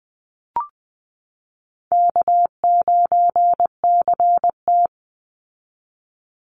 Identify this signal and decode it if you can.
Morse code, heard as a keyed tone, about 20 wpm, 700 Hz. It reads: K9CT
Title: Easy Call Signs Repeated 2x - 20wpm